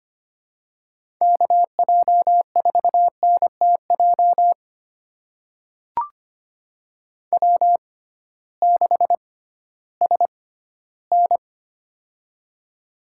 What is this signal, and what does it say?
Morse code, heard as a keyed tone, about 25 wpm, spaced as for 14 wpm, 700 Hz. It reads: KJ4NTJ W6SN
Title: USA Call Signs - 6x Character Spacing - 25wpm